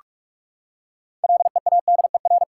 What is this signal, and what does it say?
Morse code, heard as a keyed tone, about 44 wpm, 700 Hz. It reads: LEADER
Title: Six Letter Words - 45wpm